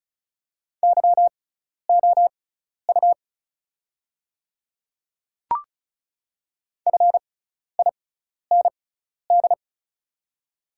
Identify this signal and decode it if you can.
Morse code, heard as a keyed tone, about 34 wpm, spaced as for 12 wpm, 700 Hz. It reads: YOU FIND
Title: 6x Character Spacing - Top 200 Words - 35wpm